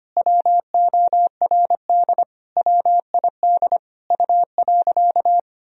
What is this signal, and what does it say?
Morse code, heard as a keyed tone, about 25 wpm, 700 Hz. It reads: WORD WID U.
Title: Sentences from Top 300 Words and Abbreviations 25wpm